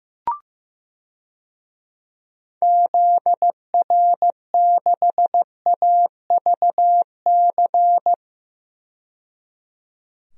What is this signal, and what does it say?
Morse code, heard as a keyed tone, about 16 wpm, 700 Hz. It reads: ZR6AVC
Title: DXCC-100 Call Signs - 6 Characters - 15wpm